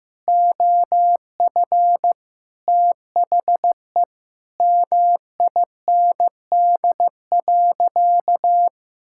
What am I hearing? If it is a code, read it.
Morse code, heard as a keyed tone, about 15 wpm, 700 Hz. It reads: OF THE MIND.
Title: Sentences from Top 1000 Words - Review Part 6 - 15wpm